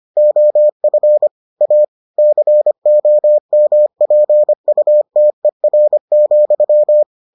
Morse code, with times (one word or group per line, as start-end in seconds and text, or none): 0.2-1.3 OF
1.6-1.8 A
2.2-7.0 COMPUTER,